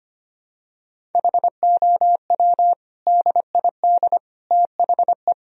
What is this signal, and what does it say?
Morse code, heard as a keyed tone, about 25 wpm, 700 Hz. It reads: HOW DID THE
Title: Sentences from Top 1000 Words - Review Part 1 - 25wpm